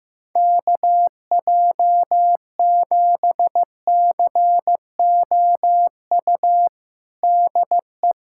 Morse code, heard as a keyed tone, about 15 wpm, 700 Hz. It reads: KJ7COU DE
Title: USA Callsign DE Callsign 15wpm